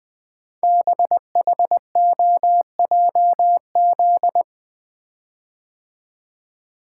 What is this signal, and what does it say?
Morse code, heard as a keyed tone, about 20 wpm, 700 Hz. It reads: BHOJZ
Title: Code Groups - 5 Letters - 20wpm